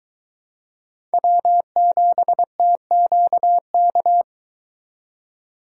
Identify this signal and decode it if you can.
Morse code, heard as a keyed tone, about 23 wpm, 700 Hz. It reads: W7TQK